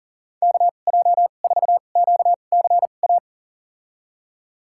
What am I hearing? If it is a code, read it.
Morse code, heard as a keyed tone, about 39 wpm, spaced as for 29 wpm, 690 Hz. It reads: KJ4QCA
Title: USA Call Signs - 6 Characters - 2x Character Spacing - 40wpm